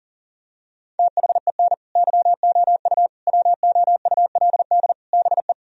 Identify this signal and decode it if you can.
Morse code, heard as a keyed tone, about 40 wpm, 700 Hz. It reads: THEN YOU WOULD BE
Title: Sentences from Top 1000 Words - Review Part 1 - 40wpm